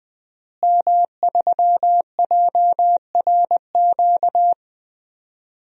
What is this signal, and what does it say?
Morse code, heard as a keyed tone, about 21 wpm, 700 Hz. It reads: M3JRQ
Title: DXCC-100 Call Signs - 5 Characters - Repeated 2x - 20wpm